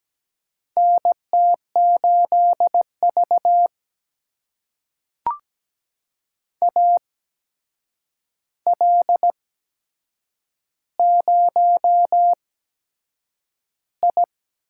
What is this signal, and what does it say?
Morse code, heard as a keyed tone, about 17 wpm, spaced as for 7 wpm, 700 Hz. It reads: NT8V AL0I
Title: USA Call Signs - 4 Characters - 8x Character Spacing - 17wpm